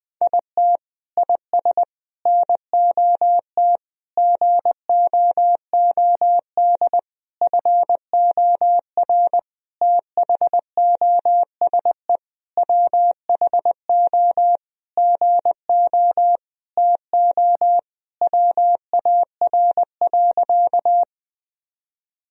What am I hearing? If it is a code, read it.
Morse code, heard as a keyed tone, about 21 wpm, 700 Hz. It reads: IT IS NOT GOOD FOR THOSE WHO GO TO WAR.
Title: Sentences from Top 1000 Words - Review Part 2 - 20wpm